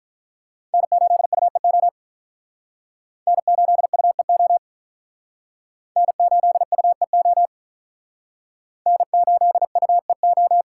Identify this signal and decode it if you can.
Morse code, heard as a keyed tone, about 48 wpm, 700 Hz. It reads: N8UEO N8UEO N8UEO N8UEO